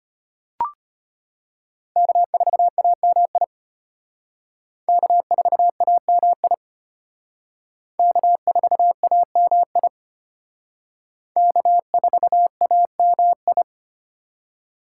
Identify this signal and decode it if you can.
Morse code, heard as a keyed tone, about 31 wpm, 700 Hz. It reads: K4AMI K4AMI K4AMI K4AMI
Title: Speed Racing - USA 1x3 Call Signs 25wpm